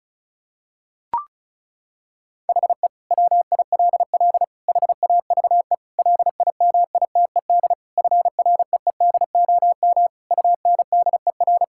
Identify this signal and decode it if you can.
Morse code, heard as a keyed tone, about 36 wpm, 700 Hz. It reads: HE WILL HAVE LIMITED FREEDOM UNDER
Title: Sentences from Top 1000 Words - Review Part 6 - 35wpm